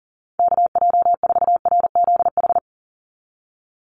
Morse code, heard as a keyed tone, about 40 wpm, 700 Hz. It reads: KJ4RZH
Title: USA Call Signs 40wpm